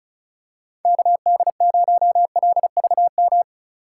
Morse code, heard as a keyed tone, about 36 wpm, 700 Hz. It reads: KD0LVM
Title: USA Call Signs - 6 Characters - 2x Character Spacing - 35wpm